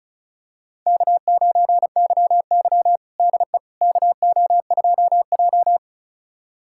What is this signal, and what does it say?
Morse code, heard as a keyed tone, about 36 wpm, 700 Hz. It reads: K9YY DE KO2J
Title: USA Callsign DE Callsign - 4 Characters - 35wpm